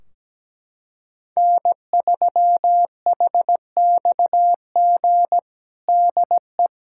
Morse code, heard as a keyed tone, about 17 wpm, 700 Hz. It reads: N3HXG DE
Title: K1USN SST QSO Practice 17wpm